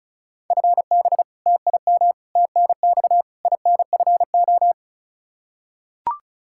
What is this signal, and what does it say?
Morse code, heard as a keyed tone, about 34 wpm, 700 Hz. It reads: FB TIM TNX INFO
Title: Practice QSO - Exchange 3 - More Info - 35wpm